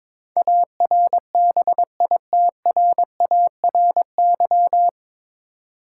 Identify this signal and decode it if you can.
Morse code, heard as a keyed tone, about 22 wpm, 700 Hz. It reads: ARBITRARY
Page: Morse Code Ninja practice 22wpm